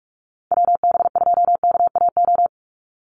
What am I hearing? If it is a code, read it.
Morse code, heard as a keyed tone, about 44 wpm, 700 Hz. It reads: WB2KAO
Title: Call Signs Repeated 3x - 45wpm